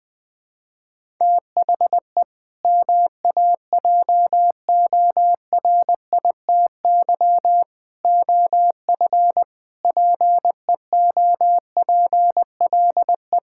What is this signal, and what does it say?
Morse code, heard as a keyed tone, about 20 wpm, 700 Hz. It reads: THE MAJORITY OF PEOPLE
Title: Sentences with 3-Word Phrases - 20wpm